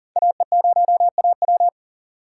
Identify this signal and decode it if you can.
Morse code, heard as a keyed tone, about 38 wpm, 700 Hz. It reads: AE0AW
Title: USA Call Signs - 5 Characters - 6x Character Spacing - 40wpm